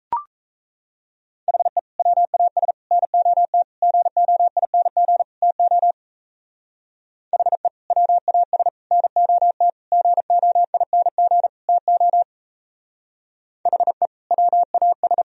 Morse code, heard as a keyed tone, about 34 wpm, 700 Hz. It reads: HE WAS NOT GOING TO HE WAS NOT GOING TO HE WAS
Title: Speed-Racing - Sets of 5 Words 28wpm